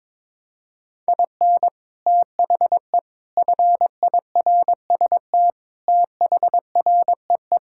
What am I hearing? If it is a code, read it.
Morse code, heard as a keyed tone, about 21 wpm, 700 Hz. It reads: IN THE FIRST THREE
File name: practice / 22wpm / Sets of 6 Words - 22wpm